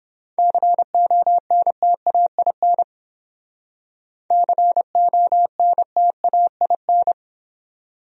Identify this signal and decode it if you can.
Morse code, heard as a keyed tone, about 30 wpm, 700 Hz. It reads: CONTAIN CONTAIN